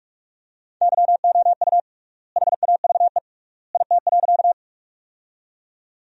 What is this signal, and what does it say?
Morse code, heard as a keyed tone, about 42 wpm, 700 Hz. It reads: YOU HAVE IT.
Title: 3x Word-Spacing - Sentences from Top 100 Words - 45wpm